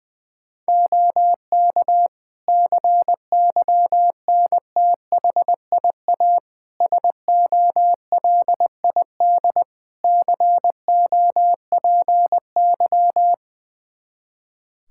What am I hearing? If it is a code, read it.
Morse code, heard as a keyed tone, about 20 wpm, 700 Hz. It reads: OK CYNTHIA SOLID COPY
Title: K1USN SST QSO Practice 20wpm